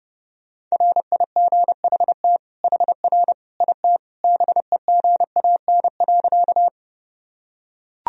Morse code, heard as a keyed tone, about 29 wpm, 700 Hz. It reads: RIGHT HR IT BEGAN.
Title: Sentences from Top 300 Words and Abbreviations 30wpm